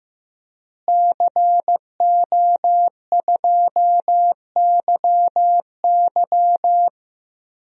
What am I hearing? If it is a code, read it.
Morse code, heard as a keyed tone, about 15 wpm, 700 Hz. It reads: CO2YY
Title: DXCC-100 Call Signs - 5 Characters - Repeated 3x - 15wpm